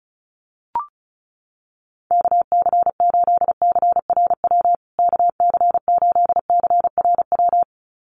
Morse code, heard as a keyed tone, about 34 wpm, 700 Hz. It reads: KC8CRW KC8CRW
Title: USA Call Signs Repeated 2x 35wpm